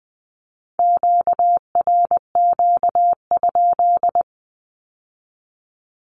Morse code, heard as a keyed tone, about 20 wpm, 700 Hz. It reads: QRQ?
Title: Q-Codes with Spoken Meaning - 20wpm